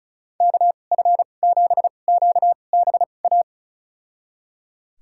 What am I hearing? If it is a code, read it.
Morse code, heard as a keyed tone, about 36 wpm, spaced as for 25 wpm, 700 Hz. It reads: KF7QBA
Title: USA Call Signs - 6 Characters - 2x Character Spacing - 35wpm